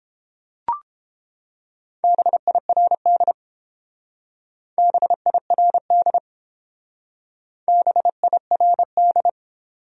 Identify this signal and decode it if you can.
Morse code, heard as a keyed tone, about 30 wpm, 700 Hz. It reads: BIRD BIRD BIRD